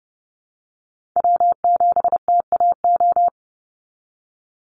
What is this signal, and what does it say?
Morse code, heard as a keyed tone, about 28 wpm, 700 Hz. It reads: W7TAO
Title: NA SOTA Chaser Callsigns 30wpm